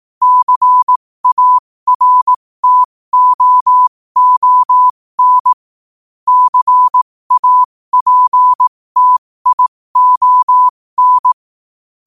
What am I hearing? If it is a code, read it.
Morse code, heard as a keyed tone, about 18 wpm, spaced as for 15 wpm, 1000 Hz. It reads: CARTOON CAPTION
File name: qotdfiles / 15wpm